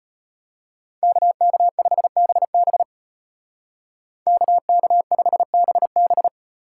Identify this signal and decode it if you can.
Morse code, heard as a keyed tone, about 35 wpm, 700 Hz. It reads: KK5BB KK5BB